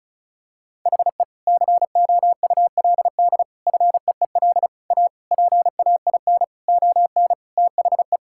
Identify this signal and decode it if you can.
Morse code, heard as a keyed tone, about 34 wpm, 700 Hz. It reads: HE COULD FEEL A PAIN ON THE
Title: Sentences from Top 1600 Words 35wpm